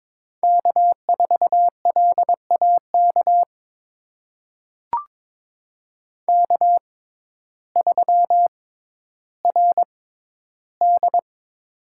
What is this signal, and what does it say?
Morse code, heard as a keyed tone, about 22 wpm, spaced as for 12 wpm, 700 Hz. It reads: K4LAK K3RD
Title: USA Call Signs - 6x Character Spacing - 22wpm